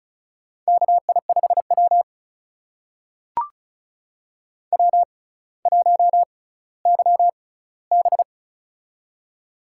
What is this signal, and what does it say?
Morse code, heard as a keyed tone, about 36 wpm, spaced as for 18 wpm, 700 Hz. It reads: KI5W W1YB
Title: USA Call Signs - 6x Character Spacing - 35wpm